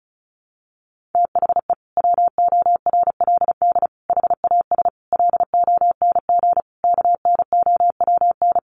Morse code, heard as a keyed tone, about 34 wpm, 700 Hz. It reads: THE WORLD HAS LONG KNOWN
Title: Sentences from Top 1200 Words 35wpm